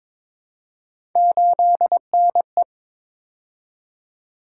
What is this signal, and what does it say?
Morse code, heard as a keyed tone, about 22 wpm, 700 Hz. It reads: 8NE
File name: practice / 22wpm / Three Letter-Number Combinations - 22wpm